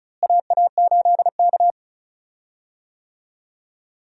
Morse code, heard as a keyed tone, about 36 wpm, 700 Hz. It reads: AA8K
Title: USA Call Signs - 4 Characters - Repeated 3x - 35wpm